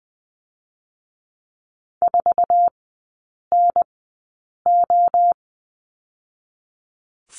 Morse code, heard as a keyed tone, about 20 wpm, 700 Hz. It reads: 4 N O
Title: 107 Lesson 18e - Letter m - Three Characters 20wpm